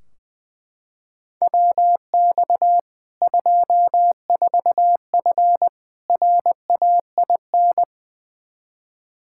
Morse code, heard as a keyed tone, about 20 wpm, 700 Hz. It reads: WX 24F RAIN
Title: Practice QSO - Exchange 2 - Info - 20wpm